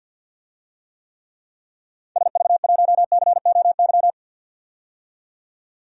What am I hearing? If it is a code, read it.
Morse code, heard as a keyed tone, about 50 wpm, 700 Hz. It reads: SV1COY
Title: DXCC-100 Call Signs - 6 Characters - 50wpm